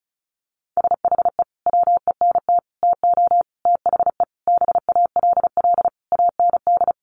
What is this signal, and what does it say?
Morse code, heard as a keyed tone, about 34 wpm, 700 Hz. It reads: SHE WENT TO THE BALL AND
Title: Sentences from Top 900 Words 35wpm